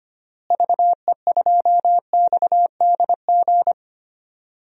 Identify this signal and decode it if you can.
Morse code, heard as a keyed tone, about 25 wpm, 700 Hz. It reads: VE2XDG
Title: DXCC-100 Call Signs - 6 Characters - Repeated 3x - 25wpm